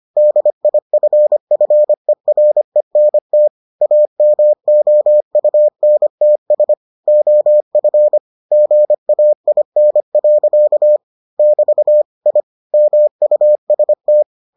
Morse code, heard as a keyed tone, about 25 wpm, 600 Hz. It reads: DIFFERENT AMOUNTS OF GAIN. <BT> I MUST